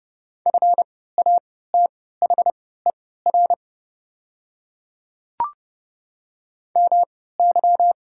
Morse code, heard as a keyed tone, about 30 wpm, spaced as for 17 wpm, 700 Hz. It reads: FATHER MY
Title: 3x Character Spacing - Top 200 Words - 30wpm